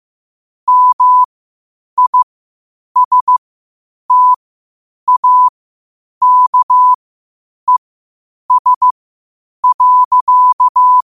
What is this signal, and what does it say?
Morse code, heard as a keyed tone, about 15 wpm, spaced as for 8 wpm, 1000 Hz. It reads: MISTAKES.
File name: qotdfiles / 8wpm